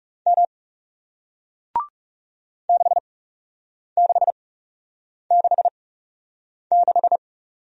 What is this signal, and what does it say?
Morse code, heard as a keyed tone, about 40 wpm, spaced as for 8 wpm, 700 Hz. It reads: M 6666